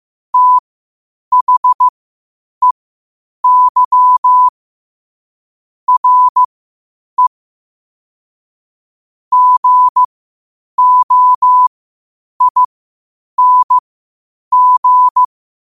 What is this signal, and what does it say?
Morse code, heard as a keyed tone, about 15 wpm, spaced as for 8 wpm, 1000 Hz. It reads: THEY RE GOING
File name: qotdfiles / 8wpm